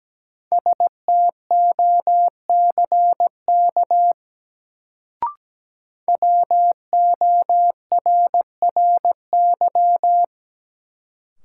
Morse code, heard as a keyed tone, about 17 wpm, 700 Hz. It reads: STOCK WORRY
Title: Sets of 5 Words - New Word Drill - 17wpm